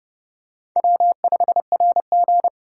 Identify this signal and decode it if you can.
Morse code, heard as a keyed tone, about 30 wpm, 700 Hz. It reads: W5RG
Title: USA Call Signs - 4 Characters - 2x Character Spacing - 30wpm